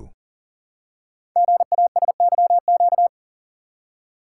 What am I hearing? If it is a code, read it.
Morse code, heard as a keyed tone, about 38 wpm, 700 Hz. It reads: GASYQ
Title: Code Groups - 5 Letter:Numbers - 40wpm